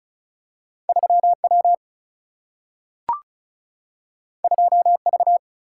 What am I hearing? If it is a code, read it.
Morse code, heard as a keyed tone, about 35 wpm, 700 Hz. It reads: 3W 2V
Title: Single-Letter Single-Number 35wpm